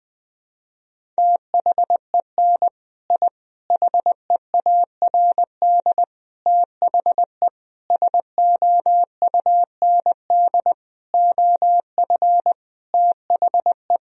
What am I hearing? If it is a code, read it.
Morse code, heard as a keyed tone, about 21 wpm, 700 Hz. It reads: THEN I HEARD THE SOUND OF THE
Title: Sentences from Top 1000 Words - Review Part 1 - 20wpm